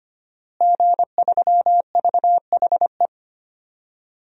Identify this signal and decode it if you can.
Morse code, heard as a keyed tone, about 25 wpm, 700 Hz. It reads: G3VHE
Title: DXCC-100 Call Signs - 5 Characters - Repeated 2x - 25wpm